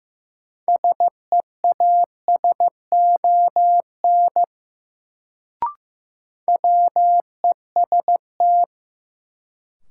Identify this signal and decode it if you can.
Morse code, heard as a keyed tone, about 15 wpm, 700 Hz. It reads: SEASON WEST
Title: Sets of 5 Words - New Word Drill - 15wpm